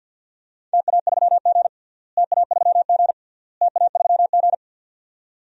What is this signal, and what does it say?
Morse code, heard as a keyed tone, about 53 wpm, 700 Hz. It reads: TA3G TA3G TA3G
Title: DXCC-100 Call Signs - 4 Characters - Repeated 3x - 50wpm